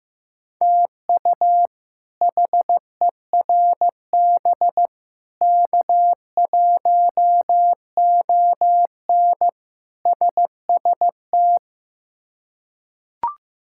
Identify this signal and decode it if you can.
Morse code, heard as a keyed tone, about 15 wpm, 700 Hz. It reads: TU HERB K1ON SST
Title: K1USN SST QSO Practice - 15wpm